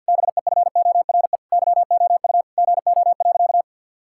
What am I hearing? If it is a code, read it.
Morse code, heard as a keyed tone, about 51 wpm, 700 Hz. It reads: BEFORE YOU GO.
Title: Sentences from Top 1000 Words - Review Part 6 - 50wpm